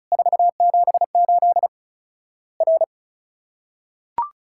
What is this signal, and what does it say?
Morse code, heard as a keyed tone, about 36 wpm, 700 Hz. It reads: 478 R
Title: Worked All Europe DX Contest - QTC Exchange - 35wpm